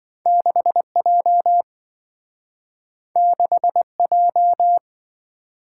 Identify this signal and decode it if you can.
Morse code, heard as a keyed tone, about 27 wpm, 700 Hz. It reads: 6J 6J